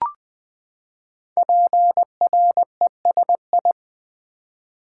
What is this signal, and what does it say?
Morse code, heard as a keyed tone, about 20 wpm, 700 Hz. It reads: PRESI
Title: Build A Word - 5 Letters At A Time - 12-Letter Words - 20wpm